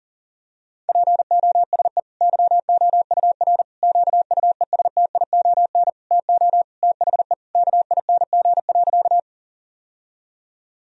Morse code, heard as a keyed tone, about 40 wpm, 700 Hz. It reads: POSE YOUR QUESTION TO THE KING.
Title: Sentences from Top 1000 Words - Review Part 2 - 40wpm